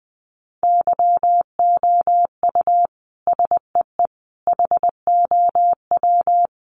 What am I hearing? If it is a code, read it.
Morse code, heard as a keyed tone, about 20 wpm, 700 Hz. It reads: YOU SEE HOW
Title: A Sound Of Thunder 20wpm